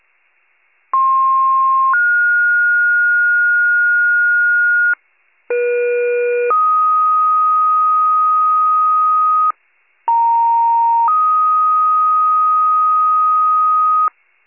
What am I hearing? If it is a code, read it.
Two-tone page, heard as a one-second tone, then a three-second tone, about 1060 then 1430 Hz, about 480 then 1250 Hz, about 930 then 1250 Hz. It is heard from 0.9 to 14.1 s.